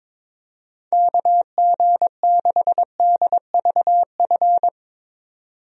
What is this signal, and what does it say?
Morse code, heard as a keyed tone, about 22 wpm, 700 Hz. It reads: KG6DVF
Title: USA Call Signs - 6 Characters - 22wpm